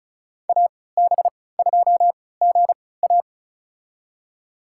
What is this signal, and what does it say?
Morse code, heard as a keyed tone, about 36 wpm, spaced as for 20 wpm, 700 Hz. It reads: AB2GA
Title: USA Call Signs - 5 Characters - 3x Character Spacing - 35wpm